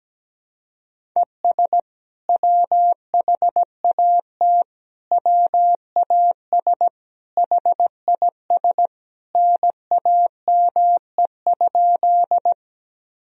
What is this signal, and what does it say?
Morse code, heard as a keyed tone, about 17 wpm, 700 Hz. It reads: ES WHAT WAS HIS NAME?
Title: Sentences from Top 200 Words and Abbreviations 17wpm